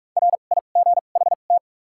Morse code, heard as a keyed tone, about 46 wpm, spaced as for 33 wpm, 700 Hz. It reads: RIGHT
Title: 2x Character Spacing - Top 300 Words - 45wpm